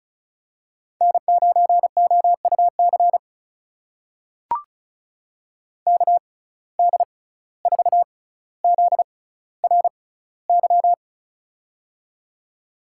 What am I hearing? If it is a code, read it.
Morse code, heard as a keyed tone, about 36 wpm, spaced as for 17 wpm, 700 Hz. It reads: N9OUC KD4ZRY
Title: USA Call Signs - 6x Character Spacing - 35wpm